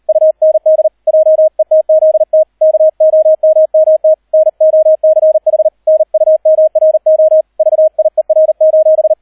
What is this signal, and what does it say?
Morse code, heard as a keyed tone, about 38 wpm, 630 Hz, with radio noise. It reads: UND JETZT KOMMT NOCH NUMRO VIER: